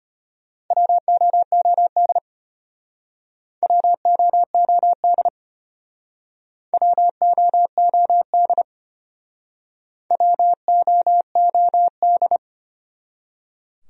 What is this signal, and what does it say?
Morse code, heard as a keyed tone, about 33 wpm, 700 Hz. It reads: WOOD WOOD WOOD WOOD